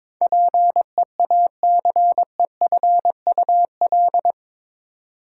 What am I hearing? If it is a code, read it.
Morse code, heard as a keyed tone, about 22 wpm, 700 Hz. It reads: PEACEFUL
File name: practice / 22wpm / Eight Letter Words - 22wpm